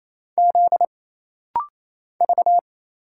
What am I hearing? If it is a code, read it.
Morse code, heard as a keyed tone, about 27 wpm, 700 Hz. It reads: Z V
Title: Single Letters - Rapid-Fire 28wpm